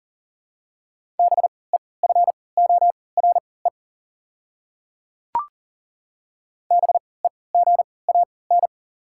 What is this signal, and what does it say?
Morse code, heard as a keyed tone, about 40 wpm, spaced as for 23 wpm, 690 Hz. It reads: BEFORE BEGAN
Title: 3x Character Spacing - Top 300 Words - 40wpm